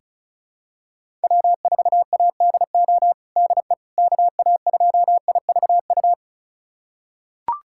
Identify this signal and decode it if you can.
Morse code, heard as a keyed tone, about 36 wpm, 700 Hz. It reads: W4ADO DE KA2IVU
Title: USA Callsign DE Callsign 35wpm